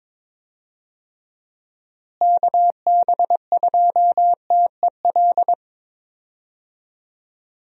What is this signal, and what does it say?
Morse code, heard as a keyed tone, about 22 wpm, 700 Hz. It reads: KB2TEL
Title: USA Call Signs - 6 Characters - Repeated 3x - 22wpm